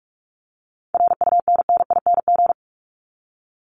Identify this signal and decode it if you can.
Morse code, heard as a keyed tone, about 44 wpm, 700 Hz. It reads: RUNNING
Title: ING Suffix - Top 100 Words with Base Words 45wpm